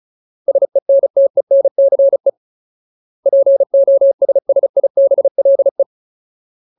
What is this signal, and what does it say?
Morse code, heard as a keyed tone, about 34 wpm, 550 Hz. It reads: SENTENCE POSSIBLE
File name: Ultimate Word List / 35WPM Ultimate